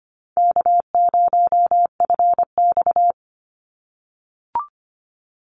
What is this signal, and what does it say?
Morse code, heard as a keyed tone, about 25 wpm, 700 Hz. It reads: K0FX
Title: Easy Call Signs - 25wpm